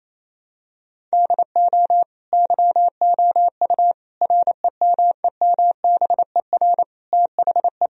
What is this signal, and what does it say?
Morse code, heard as a keyed tone, about 27 wpm, 700 Hz. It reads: DO YOU REMEMBER THE
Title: Sentences from Top 1000 Words - Review Part 1 - 28wpm